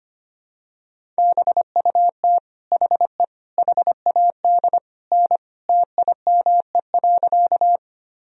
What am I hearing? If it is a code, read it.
Morse code, heard as a keyed tone, about 25 wpm, 700 Hz. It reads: BUT HE HAD N TIME.
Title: Sentences from Top 100 Words and Abbreviations 25wpm